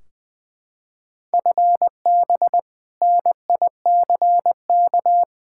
Morse code, heard as a keyed tone, about 20 wpm, 700 Hz. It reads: FB NICK
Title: Practice QSO - Exchange 3 - More Info - 20wpm